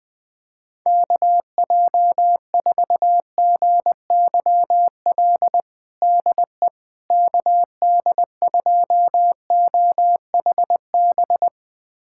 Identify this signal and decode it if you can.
Morse code, heard as a keyed tone, about 21 wpm, 700 Hz. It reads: KJ4GYL DE KD2OHB
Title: USA Callsign DE Callsign - 6 Characters - 20wpm